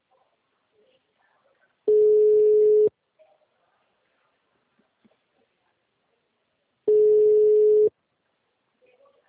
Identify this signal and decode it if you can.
Ringback tone (Europe, 425 Hz).